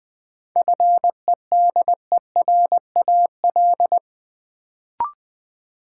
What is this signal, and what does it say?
Morse code, heard as a keyed tone, about 20 wpm, 700 Hz. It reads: FEDERAL
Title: Sets of 3 Words - New Word Drill - 20wpm